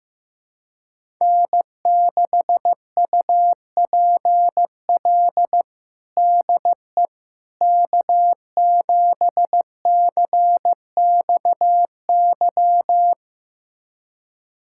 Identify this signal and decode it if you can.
Morse code, heard as a keyed tone, about 15 wpm, 700 Hz. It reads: N6UPL DE K7CXY
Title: USA Callsign DE Callsign - 5 Characters - 15wpm